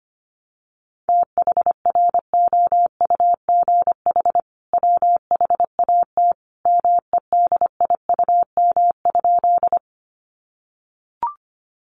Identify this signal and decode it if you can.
Morse code, heard as a keyed tone, about 25 wpm, 700 Hz. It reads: THROUGH WHAT MEDIUM?
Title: Sentences from Top 900 Words 25wpm